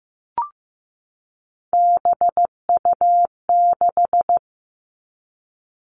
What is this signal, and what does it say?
Morse code, heard as a keyed tone, about 15 wpm, 700 Hz. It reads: BU6
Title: Three Challenging Letter-Number Combinations 15wpm